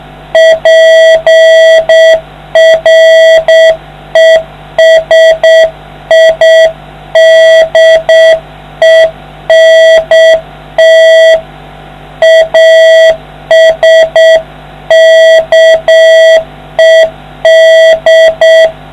Morse code, heard as a keyed tone, about 7 wpm, 650 Hz, with noise. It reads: PRESIDENT ASKED